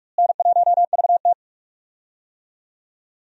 Morse code, heard as a keyed tone, about 46 wpm, 700 Hz. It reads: N1VT
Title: USA Call Signs - 4 Characters - Repeated 2x - 45wpm